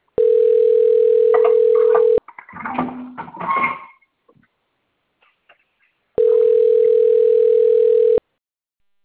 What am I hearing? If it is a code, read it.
Ringback tone (North America), 440 + 480 Hz.